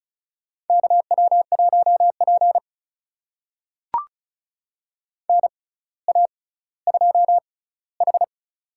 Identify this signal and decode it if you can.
Morse code, heard as a keyed tone, about 36 wpm, spaced as for 18 wpm, 700 Hz. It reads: KW1P NA2H
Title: USA Call Signs - 6x Character Spacing - 35wpm